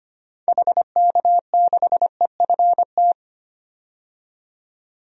Morse code, heard as a keyed tone, about 25 wpm, 700 Hz. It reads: HK6EFT
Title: DXCC-100 Call Signs - 6 Characters - 25wpm